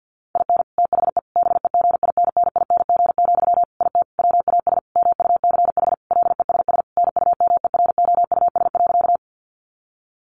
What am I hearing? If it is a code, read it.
Morse code, heard as a keyed tone, about 50 wpm, 700 Hz. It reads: IN THE BEGINNING, IT WAS MUCH LESS NUMEROUS.
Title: Sentences from Top 1400 Words 50wpm